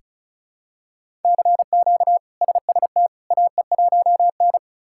Morse code, heard as a keyed tone, about 36 wpm, 700 Hz. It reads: CQ SST AE1N
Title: K1USN SST QSO Practice - 35wpm